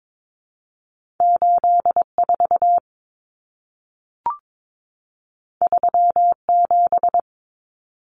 Morse code, heard as a keyed tone, about 22 wpm, 700 Hz. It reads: 84 37
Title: Numbers - 2-Digit 22wpm